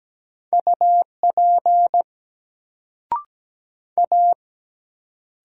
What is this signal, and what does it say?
Morse code, heard as a keyed tone, about 17 wpm, 700 Hz. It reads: UP A
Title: Sentences from Top 100 Words - New Word Drill - 17wpm